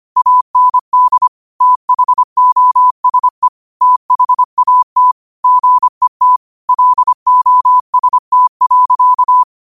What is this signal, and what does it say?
Morse code, heard as a keyed tone, about 25 wpm, 1000 Hz. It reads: AND THOSE THAT GET LOST.
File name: qotdfiles / 25wpm